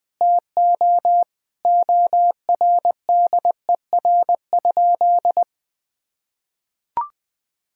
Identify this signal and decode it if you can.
Morse code, heard as a keyed tone, about 20 wpm, 700 Hz. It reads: TO ORDER?
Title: Sentences from Top 1000 Words - Review Part 2 - 20wpm